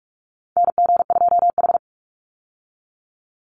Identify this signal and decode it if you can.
Morse code, heard as a keyed tone, about 44 wpm, 700 Hz. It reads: NG2H
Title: Call Signs - 45wpm